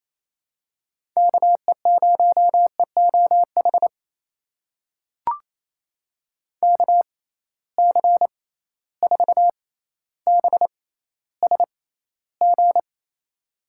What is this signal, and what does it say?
Morse code, heard as a keyed tone, about 27 wpm, spaced as for 14 wpm, 700 Hz. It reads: KE0EOH KC4BSG
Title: USA Call Signs - 6 Characters - 6x Character Spacing - 28wpm